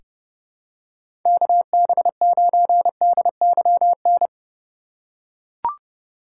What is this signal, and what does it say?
Morse code, heard as a keyed tone, about 30 wpm, 700 Hz. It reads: KB9DYN
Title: USA Call Signs - 2x Character Spacing - 30wpm